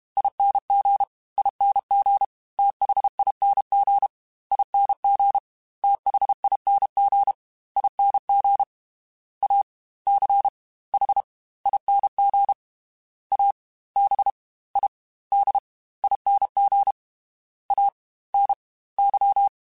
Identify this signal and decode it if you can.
Morse code, heard as a keyed tone, about 32 wpm, 800 Hz. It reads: ING ING THING ING THING ING A C H ING A B I D ING A N Y
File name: Special INGs / Special INGs-mp3s / Fast ing @30